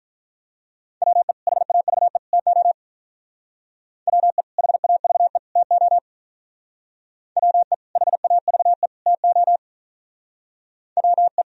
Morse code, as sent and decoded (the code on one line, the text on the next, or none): .-- . / .... .- ...- . / - --- / .-- . / .... .- ...- . / - --- / .-- . / .... .- ...- . / - --- / .-- .
WE HAVE TO WE HAVE TO WE HAVE TO WE